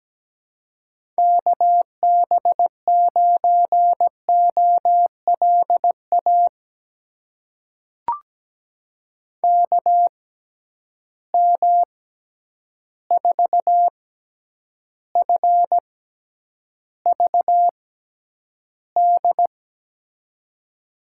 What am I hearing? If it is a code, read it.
Morse code, heard as a keyed tone, about 17 wpm, spaced as for 9 wpm, 700 Hz. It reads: KB9OLA KM4FVD
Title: USA Call Signs - 6x Character Spacing - 17wpm